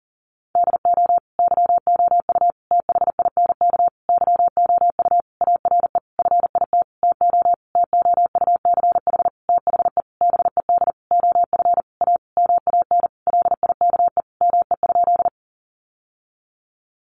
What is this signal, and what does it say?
Morse code, heard as a keyed tone, about 40 wpm, 700 Hz. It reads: DO YOU THINK YOU ARE FIT TO TOUCH THE BED OF A MAN LIKE ME?
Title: Sentences from Top 1200 Words 40wpm